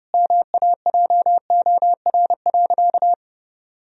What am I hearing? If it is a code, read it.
Morse code, heard as a keyed tone, about 30 wpm, 700 Hz. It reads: MAJOR.
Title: Sentences from Top 1000 Words - Review Part 4 - 30wpm